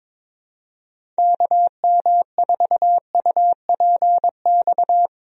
Morse code, heard as a keyed tone, about 22 wpm, 700 Hz. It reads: KM4UPX
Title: USA Call Signs - 6 Characters - Repeated 2x - 22wpm